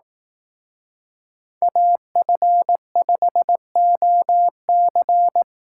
Morse code, heard as a keyed tone, about 18 wpm, 700 Hz. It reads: AF5OC
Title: Speed Racing - USA 2x2 Call Signs 15wpm